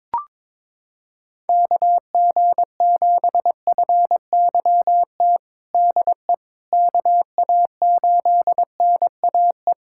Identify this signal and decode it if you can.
Morse code, heard as a keyed tone, about 22 wpm, 700 Hz. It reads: KG7FYT DE KA8NAE
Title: USA Callsign DE Callsign - 6 Characters - 22wpm